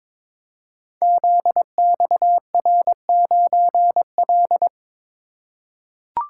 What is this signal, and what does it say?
Morse code, heard as a keyed tone, about 22 wpm, 700 Hz. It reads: ZXR9L
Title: Code Groups - 5 Letter:Numbers - 22wpm